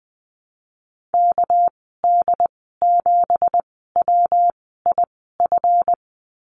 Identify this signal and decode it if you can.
Morse code, heard as a keyed tone, about 20 wpm, spaced as for 14 wpm, 700 Hz. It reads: KD7WIF
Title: 231 Lesson 37g - Letter x - Call Signs 20wpm